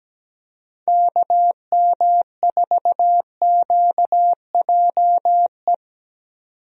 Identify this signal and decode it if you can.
Morse code, heard as a keyed tone, about 17 wpm, 700 Hz. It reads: KM4QJE
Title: USA Call Signs - 6 Characters - 17wpm